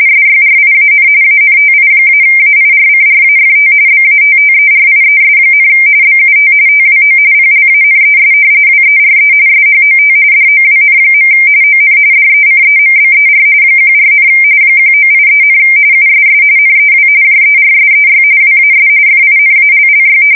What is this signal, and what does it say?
RTTY, about 45.45 baud. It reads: SYRIAN ANTI-AIRCRAFT BATTERIES, BUT TWO PLANES WERE SHOT DOWN. ON DECEMBER 14, 1983, USS NEW JERSEY WAS FINALLY
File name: fsk170 / battleshipnj